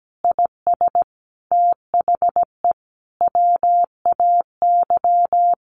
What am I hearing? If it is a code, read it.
Morse code, heard as a keyed tone, about 17 wpm, 700 Hz. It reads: IS THE WAY
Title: Sentences from Top 1500 Words 17wpm